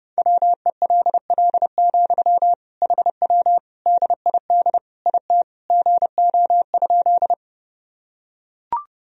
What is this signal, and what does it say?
Morse code, heard as a keyed tone, about 30 wpm, 700 Hz. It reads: WELL, HW DID IT GO?
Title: Sentences from Top 200 Words and Abbreviations 30wpm